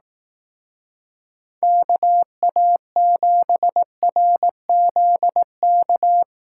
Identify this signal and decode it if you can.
Morse code, heard as a keyed tone, about 18 wpm, 700 Hz. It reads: KA7RZK
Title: Speed Racing - USA 2x3 Call Signs 15wpm